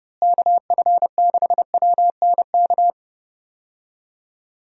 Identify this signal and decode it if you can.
Morse code, heard as a keyed tone, about 30 wpm, 700 Hz. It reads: KF6WNK
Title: USA Call Signs - 6 Characters - Repeated 3x - 30wpm